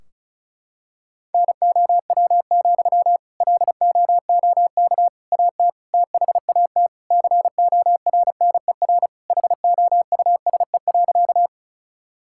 Morse code, heard as a keyed tone, about 34 wpm, 700 Hz. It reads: NOW, LOOK AT THAT CORNER HOUSE.
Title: Sentences from Top 1600 Words 35wpm